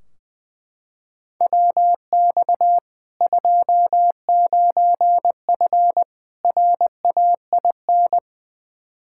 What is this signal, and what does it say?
Morse code, heard as a keyed tone, about 20 wpm, 700 Hz. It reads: WX 29F RAIN
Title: K1USN SST QSO Practice 20wpm